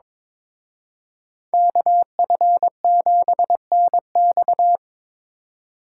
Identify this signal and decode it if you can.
Morse code, heard as a keyed tone, about 27 wpm, 700 Hz. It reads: KF7NX
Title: Speed Racing - USA Call Signs 22wpm